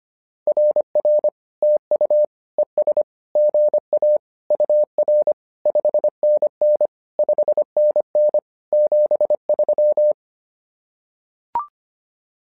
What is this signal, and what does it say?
Morse code, heard as a keyed tone, about 25 wpm, 600 Hz. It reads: RR TU ES GA UR 5NN 5NN 73
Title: SOTA QSO Practice 25wpm